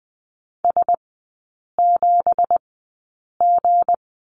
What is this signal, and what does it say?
Morse code, heard as a keyed tone, about 20 wpm, spaced as for 8 wpm, 700 Hz. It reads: S7G
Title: 184 Lesson 30e - Letter b - Three Characters 20wpm